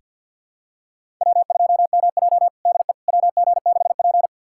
Morse code, heard as a keyed tone, about 46 wpm, 700 Hz. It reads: W2MJ DE WG6P
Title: USA Callsign DE Callsign - 4 Characters - 50wpm